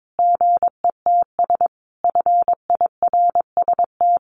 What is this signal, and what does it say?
Morse code, heard as a keyed tone, about 22 wpm, 700 Hz. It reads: GETS FIRST
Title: Additional Easy Phrases 22wpm